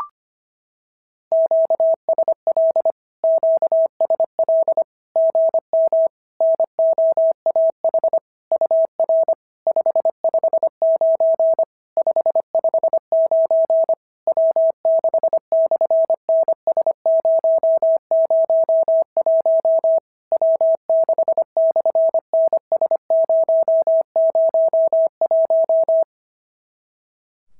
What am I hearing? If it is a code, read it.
Morse code, heard as a keyed tone, about 25 wpm, 650 Hz. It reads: QSL QSL GM NOAH UR 559 559 W6/NS001 W6/NS001
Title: SOTA QSO Practice 25wpm